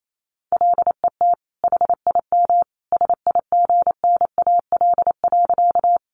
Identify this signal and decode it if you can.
Morse code, heard as a keyed tone, about 28 wpm, 700 Hz. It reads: LET HIM SIGNAL.
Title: Sentences from Top 1400 Words 28wpm